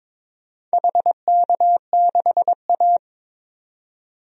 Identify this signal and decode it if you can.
Morse code, heard as a keyed tone, about 22 wpm, 700 Hz. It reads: HK6A
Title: DXCC-100 Call Signs - 4 Characters - 22wpm